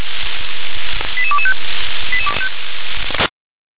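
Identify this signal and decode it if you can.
Five-tone selective call (ZVEI), heard as quick runs of five tones, tones of about 70 ms. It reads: 88225
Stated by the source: ZVEI five-tone selective call 88222, received by SDR.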